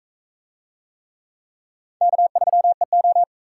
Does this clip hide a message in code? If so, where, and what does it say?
Yes: Morse code from 2.0 to 3.2 s, heard as a keyed tone, about 39 wpm, 700 Hz. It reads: K3EO